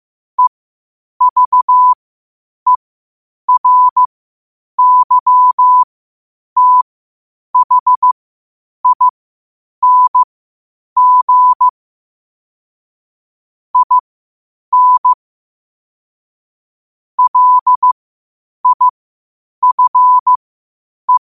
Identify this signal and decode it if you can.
Morse code, heard as a keyed tone, about 15 wpm, spaced as for 8 wpm, 1000 Hz. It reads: EVERYTHING IN LIFE
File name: qotdfiles / 8wpm